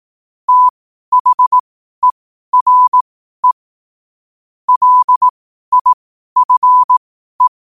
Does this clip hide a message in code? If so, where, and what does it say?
Yes: Morse code from 0.5 to 7.5 s, heard as a keyed tone, about 18 wpm, spaced as for 12 wpm, 1000 Hz. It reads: THERE LIFE